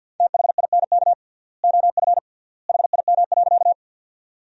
Morse code, heard as a keyed tone, about 50 wpm, 700 Hz. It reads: THINK OF HIM.
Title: 3x Word-Spacing - Sentences from Top 200 Words - 50wpm